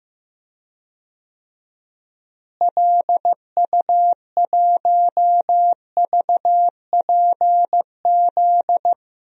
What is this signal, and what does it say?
Morse code, heard as a keyed tone, about 15 wpm, 700 Hz. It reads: LU1VPZ
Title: DXCC-100 Call Signs - 6 Characters - Repeated 2x - 15wpm